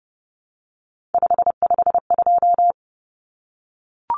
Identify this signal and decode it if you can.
Morse code, heard as a keyed tone, about 30 wpm, 700 Hz. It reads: 552
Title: Numbers - 3-Digit 30wpm